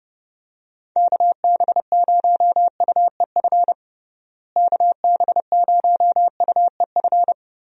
Morse code, heard as a keyed tone, about 30 wpm, 700 Hz. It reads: KB0UEF KB0UEF
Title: USA Call Signs - 6 Characters - Repeated 2x - 30wpm